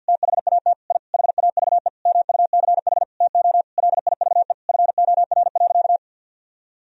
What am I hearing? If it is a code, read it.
Morse code, heard as a keyed tone, about 52 wpm, 700 Hz. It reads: THAT I HAVE MUCH TO LIVE FOR.
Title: Sentences from Top 1000 Words - Review Part 3 - 50wpm